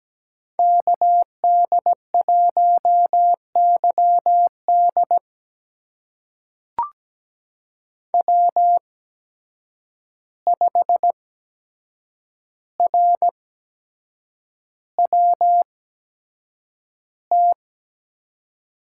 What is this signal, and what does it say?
Morse code, heard as a keyed tone, about 17 wpm, spaced as for 7 wpm, 700 Hz. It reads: KD1YD W5RWT
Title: USA Call Signs - 5 Characters - 8x Character Spacing - 17wpm